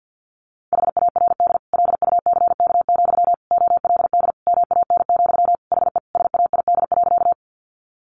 Morse code, heard as a keyed tone, about 50 wpm, 700 Hz. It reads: HARD LUCK, OLD MAN, HE SAID.
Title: Sentences from Top 1500 Words 50wpm